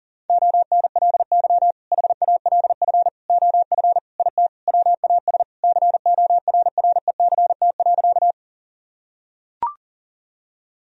Morse code, heard as a keyed tone, about 37 wpm, 700 Hz. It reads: ONLY HALF OF IT WAS CORRECT.
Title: Sentences from Top 1000 Words - Review Part 2 - 40wpm